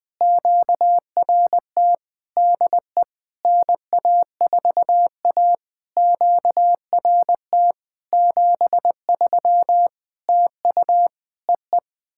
Morse code, heard as a keyed tone, about 20 wpm, 700 Hz. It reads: QRT DE NA4A QRT 73 TU EE
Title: POTA QSO Practice - 20wpm